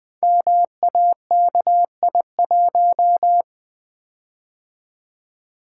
Morse code, heard as a keyed tone, about 20 wpm, 700 Hz. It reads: MAKI1
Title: Code Groups - 5 Letter:Numbers - 20wpm